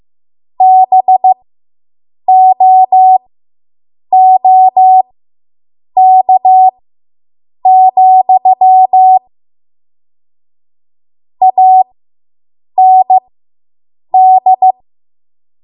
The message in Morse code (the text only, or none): BOOK, AND